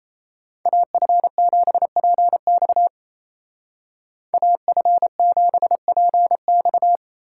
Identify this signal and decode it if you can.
Morse code, heard as a keyed tone, about 34 wpm, 700 Hz. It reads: AF7PX AF7PX
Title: Speed Racing - USA 2x2 Call Signs 28wpm